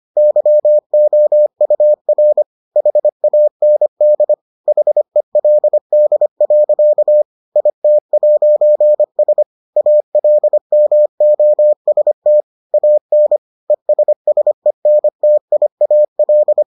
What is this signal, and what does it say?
Morse code, heard as a keyed tone, about 25 wpm, 600 Hz. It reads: YOUR HAND HELD. IT'S ALMOST AN ESSENTIAL